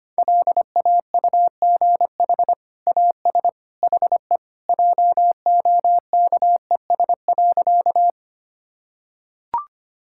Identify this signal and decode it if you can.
Morse code, heard as a keyed tone, about 25 wpm, 700 Hz. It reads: LAUGH AS HE JOKES.
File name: practice / 25wpm / Sentences with 2-Word Phrases - 25wpm